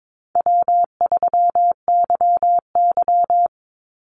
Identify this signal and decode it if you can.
Morse code, heard as a keyed tone, about 22 wpm, 700 Hz. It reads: W3YY
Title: Call Signs Repeated 2x - 22wpm